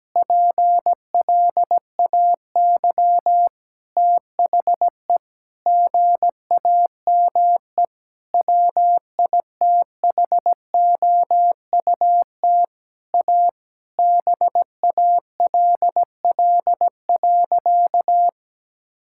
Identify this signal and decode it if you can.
Morse code, heard as a keyed tone, about 17 wpm, 700 Hz. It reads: PLAY THE GAME WITHOUT A BALL.
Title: Sentences from Top 1000 Words - Review Part 6 - 17wpm